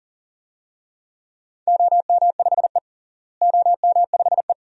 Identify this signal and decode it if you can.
Morse code, heard as a keyed tone, about 40 wpm, 700 Hz. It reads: OM5E OM5E
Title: DXCC-100 Call Signs - 4 Characters - Repeated 2x - 40wpm